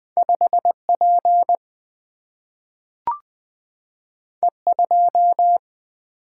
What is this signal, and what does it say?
Morse code, heard as a keyed tone, about 20 wpm, 700 Hz. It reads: 5 P E 2
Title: Single-Letter Single-Number 20wpm